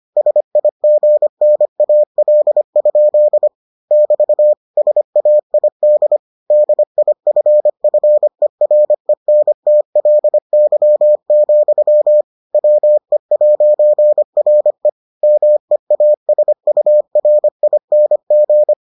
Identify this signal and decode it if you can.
Morse code, heard as a keyed tone, about 25 wpm, 600 Hz. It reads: SIGNAL? <BT> SAID DIFFERENTLY, WE'RE MEASURING